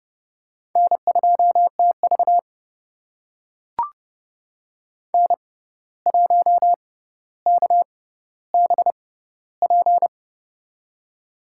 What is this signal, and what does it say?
Morse code, heard as a keyed tone, about 30 wpm, spaced as for 14 wpm, 700 Hz. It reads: N2TV N1KBP
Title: USA Call Signs - 6x Character Spacing - 30wpm